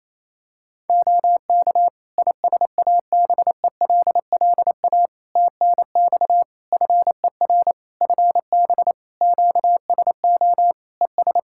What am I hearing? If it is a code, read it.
Morse code, heard as a keyed tone, about 28 wpm, 700 Hz. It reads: OK ISABELLA TNX FER FB QSO ES
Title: Practice QSO - Exchange 4 - Ending - 28wpm